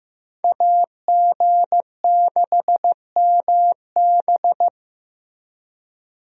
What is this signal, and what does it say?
Morse code, heard as a keyed tone, about 16 wpm, 700 Hz. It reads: AG6MB
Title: USA Call Signs - 5 Characters - Repeated 2x - 15wpm